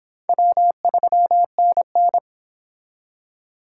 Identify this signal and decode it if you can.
Morse code, heard as a keyed tone, about 30 wpm, 700 Hz. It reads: W3NN